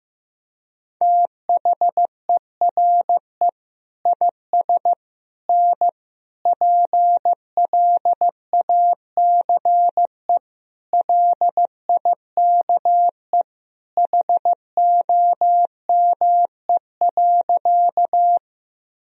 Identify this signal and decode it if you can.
Morse code, heard as a keyed tone, about 15 wpm, 700 Hz. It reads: THERE IS N PLACE LIKE HOME.
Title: Sentences from Top 200 Words and Abbreviations 15wpm